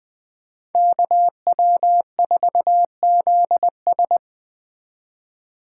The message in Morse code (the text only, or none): KW4ZS